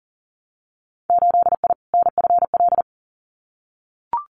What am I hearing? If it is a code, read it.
Morse code, heard as a keyed tone, about 40 wpm, 700 Hz. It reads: 8I NFL
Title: Winter Field Day Exchanges Encore 40wpm